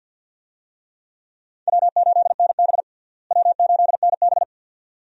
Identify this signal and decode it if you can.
Morse code, heard as a keyed tone, about 51 wpm, 700 Hz. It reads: W8NB W8NB
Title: USA Call Signs - 4 Characters - Repeated 2x - 50wpm